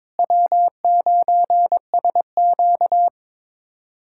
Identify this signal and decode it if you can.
Morse code, heard as a keyed tone, about 21 wpm, 700 Hz. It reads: W9SQ
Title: USA Callsign DE Callsign - 4 Characters - 22wpm